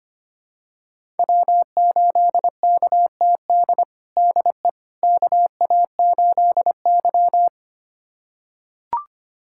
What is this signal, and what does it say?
Morse code, heard as a keyed tone, about 25 wpm, 700 Hz. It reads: W8KTD DE KA8Y
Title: USA Callsign DE Callsign 25wpm